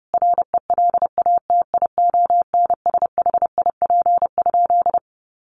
Morse code, heard as a keyed tone, about 30 wpm, 700 Hz. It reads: RELATIONSHIP?
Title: Sentences from Top 1100 Words 30wpm